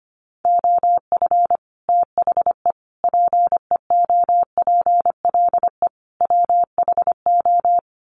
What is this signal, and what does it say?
Morse code, heard as a keyed tone, about 25 wpm, 700 Hz. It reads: OF THE PEOPLE WHO
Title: Sets of 4 Words 25wpm